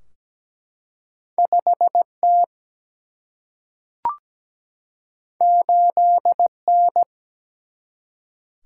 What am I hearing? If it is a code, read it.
Morse code, heard as a keyed tone, about 17 wpm, 700 Hz. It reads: 5 T 8 N
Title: Single-Letter Single-Number 17wpm